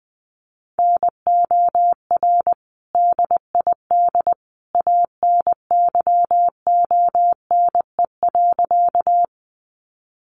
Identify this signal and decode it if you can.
Morse code, heard as a keyed tone, about 20 wpm, 700 Hz. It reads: NOR DID ANYONE.
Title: Sentences from Top 1500 Words 20wpm